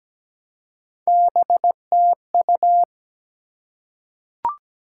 Morse code, heard as a keyed tone, about 17 wpm, 700 Hz. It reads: BTU
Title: Top 90 QSO Elements 17wpm